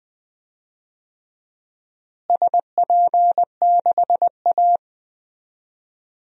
Morse code, heard as a keyed tone, about 20 wpm, 700 Hz. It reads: SP6A
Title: DXCC-100 Call Signs - 4 Characters - Repeated 2x - 20wpm